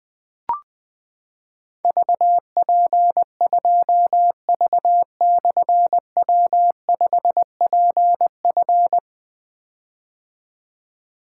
Morse code, heard as a keyed tone, about 20 wpm, 700 Hz. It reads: VP2V/W5PF
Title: Difficult Call Signs 20wpm